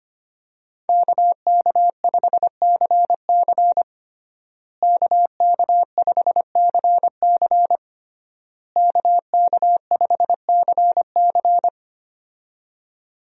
Morse code, heard as a keyed tone, about 25 wpm, 700 Hz. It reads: KK5CC KK5CC KK5CC
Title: USA Call Signs - 5 Characters - Repeated 3x - 25wpm